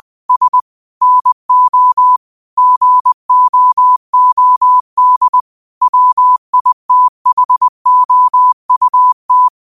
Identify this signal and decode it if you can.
Morse code, heard as a keyed tone, about 20 wpm, 1000 Hz. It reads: S NO GOOD WITHOUT